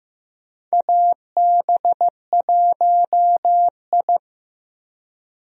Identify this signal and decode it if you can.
Morse code, heard as a keyed tone, about 15 wpm, 700 Hz. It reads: AB1I
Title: USA Call Signs - 4 Characters - Repeated 3x - 15wpm